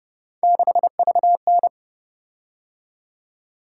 Morse code, heard as a keyed tone, about 30 wpm, 700 Hz. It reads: 6VN
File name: Speed-Racing - Three Letter-Number Combinations - 25wpm